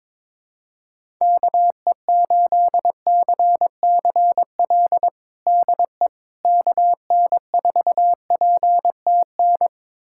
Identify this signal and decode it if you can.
Morse code, heard as a keyed tone, about 22 wpm, 700 Hz. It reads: KE8CCL DE KN4PTN
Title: USA Callsign DE Callsign - 6 Characters - 22wpm